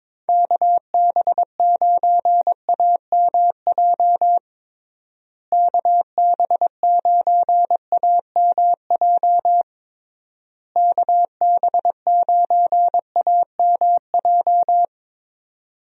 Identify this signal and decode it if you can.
Morse code, heard as a keyed tone, about 22 wpm, 700 Hz. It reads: KB9AMJ KB9AMJ KB9AMJ
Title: USA Call Signs - 6 Characters - Repeated 3x - 22wpm